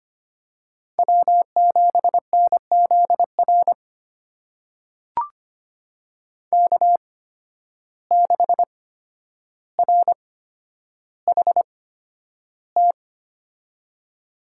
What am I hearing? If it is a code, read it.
Morse code, heard as a keyed tone, about 25 wpm, spaced as for 11 wpm, 700 Hz. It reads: W7NZR K6RHT
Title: USA Call Signs - 8x Character Spacing - 25wpm